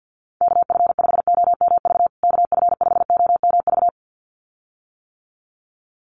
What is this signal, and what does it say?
Morse code, heard as a keyed tone, about 50 wpm, 700 Hz. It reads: KF5OMV KF5OMV
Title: USA Call Signs Repeated 2x 50wpm